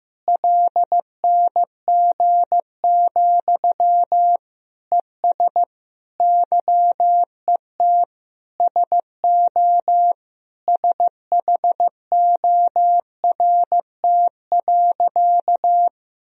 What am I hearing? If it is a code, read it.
Morse code, heard as a keyed tone, about 15 wpm, 700 Hz. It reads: LNG, ES YET SO SHORT.
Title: Sentences from Top 500 Words and Abbreviations 15wpm